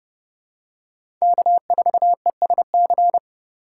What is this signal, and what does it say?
Morse code, heard as a keyed tone, about 30 wpm, 700 Hz. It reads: K4ESC
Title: USA Call Signs - 5 Characters - 2x Character Spacing - 30wpm